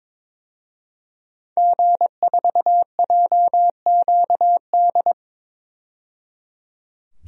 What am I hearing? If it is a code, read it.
Morse code, heard as a keyed tone, about 22 wpm, 700 Hz. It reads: G4JQD
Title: DXCC-100 Call Signs - 5 Characters - 22wpm